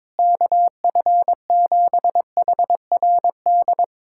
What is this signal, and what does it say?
Morse code, heard as a keyed tone, about 22 wpm, 700 Hz. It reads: KF7HRD
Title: USA Callsign DE Callsign - 6 Characters - 22wpm